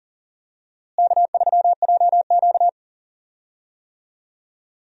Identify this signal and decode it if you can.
Morse code, heard as a keyed tone, about 37 wpm, 700 Hz. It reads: K3JQ
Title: USA Call Signs - 4 Characters - Repeated 3x - 40wpm